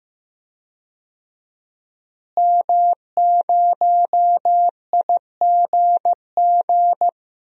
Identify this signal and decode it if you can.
Morse code, heard as a keyed tone, about 15 wpm, 700 Hz. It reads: M0IGG
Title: DXCC-100 Call Signs - 5 Characters - Repeated 2x - 15wpm